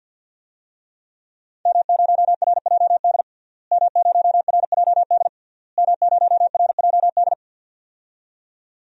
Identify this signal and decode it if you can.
Morse code, heard as a keyed tone, about 50 wpm, 700 Hz. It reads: M0RJD M0RJD M0RJD
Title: DXCC-100 Call Signs - 5 Characters - Repeated 3x - 50wpm